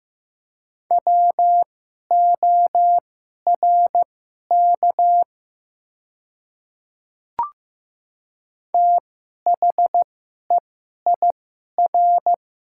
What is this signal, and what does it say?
Morse code, heard as a keyed tone, about 15 wpm, spaced as for 11 wpm, 700 Hz. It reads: WORK THEIR
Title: 2x Character Spacing - Top 200 Words - 15wpm